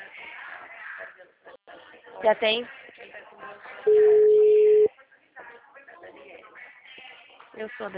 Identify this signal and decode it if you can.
Ringback tone (Europe, 425 Hz), with speech.
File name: out- -2045- - - .11425